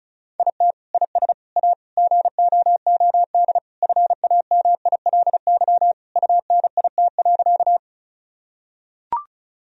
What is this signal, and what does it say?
Morse code, heard as a keyed tone, about 36 wpm, 700 Hz. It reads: IT IS A GOOD FAMILY UNIT.
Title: Sentences from Top 1000 Words - Review Part 2 - 35wpm